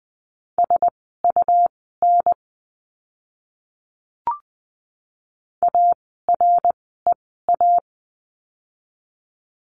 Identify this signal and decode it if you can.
Morse code, heard as a keyed tone, about 20 wpm, spaced as for 15 wpm, 700 Hz. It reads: SUN AREA
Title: 101 Lesson 17f - Letter u - Words 20wpm